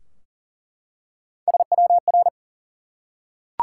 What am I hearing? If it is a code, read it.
Morse code, heard as a keyed tone, about 39 wpm, 700 Hz. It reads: SWR
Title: Top 180 QSO Elements 40wpm